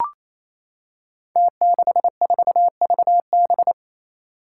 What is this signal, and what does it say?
Morse code, heard as a keyed tone, about 28 wpm, 700 Hz. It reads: T64VB
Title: Five Challenging Letter-Number Combinations 28wpm